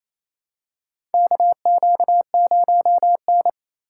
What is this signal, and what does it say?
Morse code, heard as a keyed tone, about 28 wpm, 700 Hz. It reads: KQ0N
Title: USA Call Signs - 4 Characters - 3x Character Spacing - 28wpm